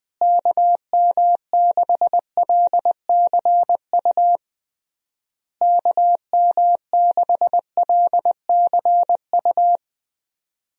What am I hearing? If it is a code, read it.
Morse code, heard as a keyed tone, about 21 wpm, 700 Hz. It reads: KM6LCU KM6LCU
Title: USA Call Signs - 6 Characters - Repeated 3x - 20wpm